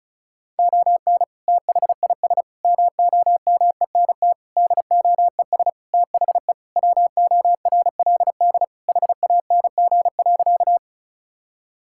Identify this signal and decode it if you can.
Morse code, heard as a keyed tone, about 34 wpm, 700 Hz. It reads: ON THIS MOMENT DOES THE WORLD HANG.
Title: Sentences from Top 1600 Words 35wpm